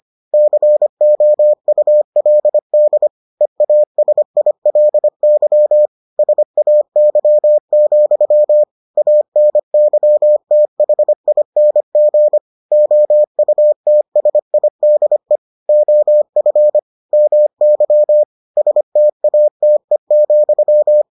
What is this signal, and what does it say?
Morse code, heard as a keyed tone, about 25 wpm, 600 Hz. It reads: COULD EASILY SAY, ANYTHING OUTSIDE OF MY STATE,